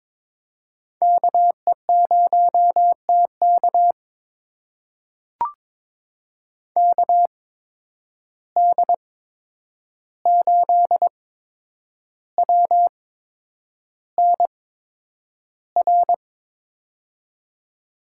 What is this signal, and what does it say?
Morse code, heard as a keyed tone, about 22 wpm, spaced as for 9 wpm, 700 Hz. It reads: KE0TK KD8WNR
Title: USA Call Signs - 8x Character Spacing - 22wpm